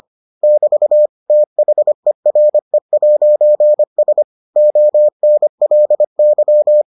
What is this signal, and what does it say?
Morse code, heard as a keyed tone, about 25 wpm, 600 Hz. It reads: <BT> THERE'S ONLY